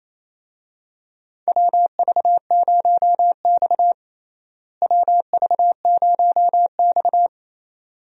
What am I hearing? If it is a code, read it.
Morse code, heard as a keyed tone, about 28 wpm, 700 Hz. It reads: WV0X WV0X
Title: USA Call Signs - 4 Characters - Repeated 2x - 28wpm